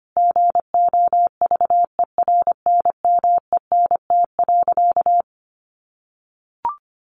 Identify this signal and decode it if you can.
Morse code, heard as a keyed tone, about 25 wpm, 700 Hz. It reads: GOVERNMENT.
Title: Sentences from Top 1400 Words 25wpm